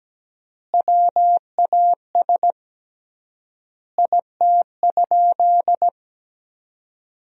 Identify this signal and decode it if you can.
Morse code, heard as a keyed tone, about 17 wpm, 700 Hz. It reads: WAS IT?
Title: 3x Word-Spacing - Sentences from Top 100 Words - 17wpm